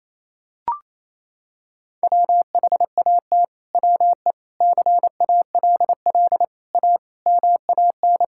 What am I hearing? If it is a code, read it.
Morse code, heard as a keyed tone, about 28 wpm, 700 Hz. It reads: WHAT WE CALL A MAN
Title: Sentences from Top 1600 Words 28wpm